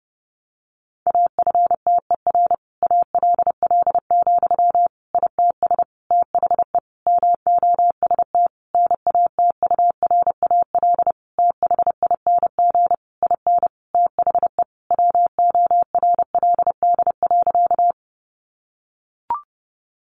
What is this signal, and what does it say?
Morse code, heard as a keyed tone, about 30 wpm, 700 Hz. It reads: AFTER ALL, ITS THE MOST NATURAL THING IN THE WORLD.
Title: Sentences from Top 700 Words 30wpm